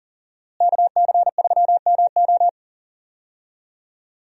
Morse code, heard as a keyed tone, about 40 wpm, 700 Hz. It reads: KC3MO
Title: USA Call Signs - 5 Characters - Repeated 3x - 40wpm